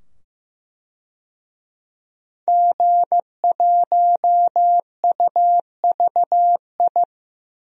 Morse code, heard as a keyed tone, about 15 wpm, 700 Hz. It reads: G1UVI
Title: DXCC-100 Call Signs - 5 Characters - 15wpm